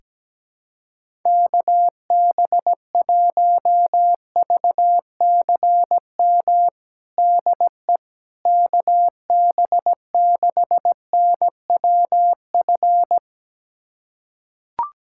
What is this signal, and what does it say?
Morse code, heard as a keyed tone, about 17 wpm, 700 Hz. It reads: KB1VCM DE KB6NWF
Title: USA Callsign DE Callsign - 6 Characters - 17wpm